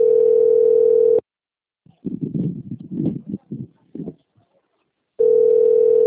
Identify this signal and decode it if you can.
Ringback tone (North America), 440 + 480 Hz.
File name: out- - - - - .16876